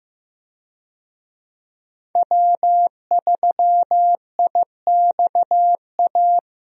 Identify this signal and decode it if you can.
Morse code, heard as a keyed tone, about 15 wpm, 700 Hz. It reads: W3IXA
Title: USA Call Signs - 5 Characters - Repeated 3x - 15wpm